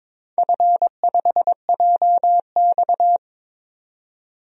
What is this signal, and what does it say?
Morse code, heard as a keyed tone, about 22 wpm, 700 Hz. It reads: F5JX
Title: DXCC-100 Call Signs - 4 Characters - 22wpm